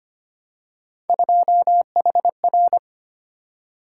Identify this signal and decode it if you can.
Morse code, heard as a keyed tone, about 25 wpm, 700 Hz. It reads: 2HR
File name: Three Letter-Number Combinations - 25wpm